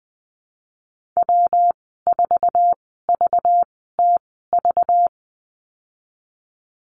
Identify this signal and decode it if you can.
Morse code, heard as a keyed tone, about 20 wpm, spaced as for 15 wpm, 700 Hz. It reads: W4VTV
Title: 207 Lesson 33g - Letter j - Call Signs 20wpm